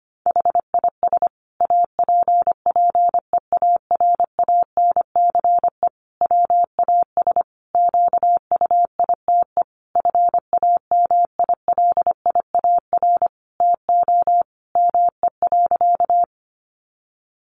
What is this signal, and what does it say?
Morse code, heard as a keyed tone, about 25 wpm, 700 Hz. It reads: HIS APPEARANCE WAS QUITE FAMILIAR TO ME.
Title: Sentences from Top 1100 Words 25wpm